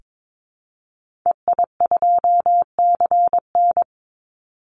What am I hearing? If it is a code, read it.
Morse code, heard as a keyed tone, about 22 wpm, 700 Hz. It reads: EI2CN
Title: Call Signs Repeated 3x - 22wpm